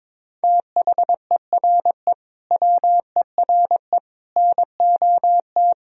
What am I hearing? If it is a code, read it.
Morse code, heard as a keyed tone, about 22 wpm, 700 Hz. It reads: THERE WERE NOT
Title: Sentences from Top 1000 Words - Review Part 1 - 22wpm